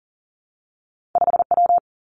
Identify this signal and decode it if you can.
Morse code, heard as a keyed tone, about 42 wpm, 700 Hz. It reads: 5 W
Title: Two Challenging Letter-Number Combinations 40wpm